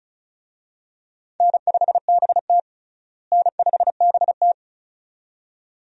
Morse code, heard as a keyed tone, about 36 wpm, 700 Hz. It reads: N5BT N5BT
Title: USA Call Signs - 4 Characters - Repeated 2x - 35wpm